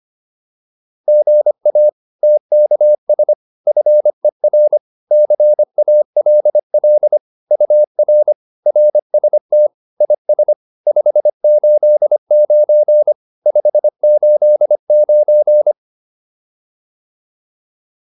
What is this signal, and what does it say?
Morse code, heard as a keyed tone, about 25 wpm, 600 Hz. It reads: GA TKS FER CALL UR RST IS 589 589